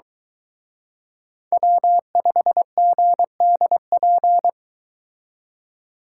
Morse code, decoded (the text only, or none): W5GDP